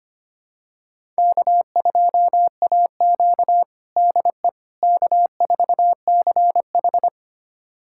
Morse code, heard as a keyed tone, about 25 wpm, 700 Hz. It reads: K2AQ DE K4CH
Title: USA Callsign DE Callsign - 4 Characters - 25wpm